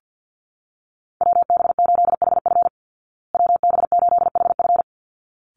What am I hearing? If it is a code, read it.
Morse code, heard as a keyed tone, about 50 wpm, 700 Hz. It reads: WB8HF WB8HF
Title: Call Signs Repeated 3x - 50wpm